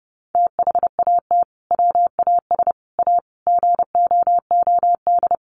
Morse code, heard as a keyed tone, about 29 wpm, 700 Hz. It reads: THAT WAS A GOOD
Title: Sentences from Top 900 Words 30wpm